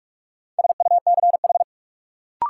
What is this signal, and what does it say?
Morse code, heard as a keyed tone, about 46 wpm, 700 Hz. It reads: SUCH
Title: Sentences from Top 200 Words - New Word Drill - 45wpm